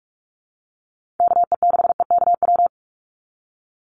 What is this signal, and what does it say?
Morse code, heard as a keyed tone, about 44 wpm, 700 Hz. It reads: KE6EKW
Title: USA Call Signs 45wpm